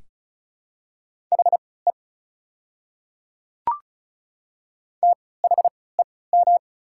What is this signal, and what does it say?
Morse code, heard as a keyed tone, about 36 wpm, spaced as for 20 wpm, 690 Hz. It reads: HE THEM
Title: 3x Character Spacing - Top 100 Words - 35wpm